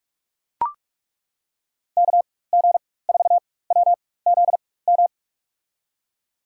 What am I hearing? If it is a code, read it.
Morse code, heard as a keyed tone, about 41 wpm, spaced as for 20 wpm, 700 Hz. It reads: KG4WZM
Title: USA Call Signs - 4x Character Spacing - 45wpm